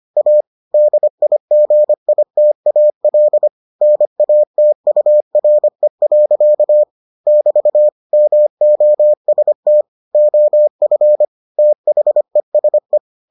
Morse code, heard as a keyed tone, about 25 wpm, 600 Hz. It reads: A DIGITAL NATURE. <BT> MOST OF THESE